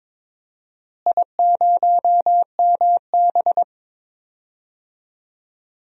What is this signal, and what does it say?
Morse code, heard as a keyed tone, about 21 wpm, 700 Hz. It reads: I0MB
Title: DXCC-100 Call Signs - 4 Characters - Repeated 3x - 22wpm